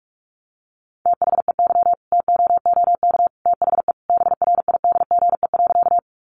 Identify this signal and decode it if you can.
Morse code, heard as a keyed tone, about 44 wpm, 700 Hz. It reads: THEY TOOK THE BRIDGE.
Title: Sentences from Top 1200 Words 45wpm